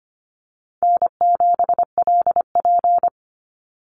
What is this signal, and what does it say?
Morse code, heard as a keyed tone, about 25 wpm, 700 Hz. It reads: N7LP
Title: NA SOTA Chaser Callsigns 25wpm